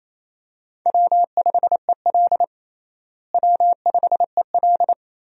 Morse code, heard as a keyed tone, about 28 wpm, 700 Hz. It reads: W5EL W5EL
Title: USA Call Signs - 4 Characters - Repeated 3x - 28wpm